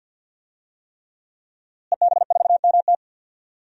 Morse code, heard as a keyed tone, about 45 wpm, 700 Hz. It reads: EB4MT
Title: DXCC-100 Call Signs - 5 Characters - Repeated 3x - 50wpm